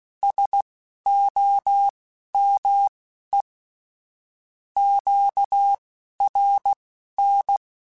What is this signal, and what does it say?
Morse code, heard as a keyed tone, about 16 wpm, spaced as for 11 wpm, 800 Hz. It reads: SOME QRN
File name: QSOs / QSOs-mp3s / WD0BGP 1@13